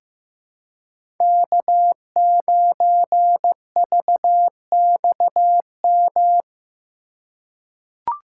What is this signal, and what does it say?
Morse code, heard as a keyed tone, about 15 wpm, 700 Hz. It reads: K9VXM
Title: USA Call Signs - 5 Characters - 2x Character Spacing - 15wpm